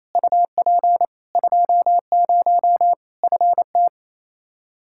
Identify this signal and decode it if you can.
Morse code, heard as a keyed tone, about 28 wpm, 700 Hz. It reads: UP 20 FT
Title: K1USN SST QSO Practice 28wpm